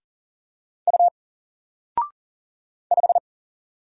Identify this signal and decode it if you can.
Morse code, heard as a keyed tone, about 40 wpm, 700 Hz. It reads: U 5
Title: Single Letter-Number Alt 40wpm